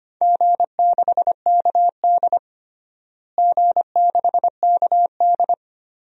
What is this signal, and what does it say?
Morse code, heard as a keyed tone, about 25 wpm, 700 Hz. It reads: G6KD G6KD
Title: DXCC-100 Call Signs - 4 Characters - Repeated 3x - 25wpm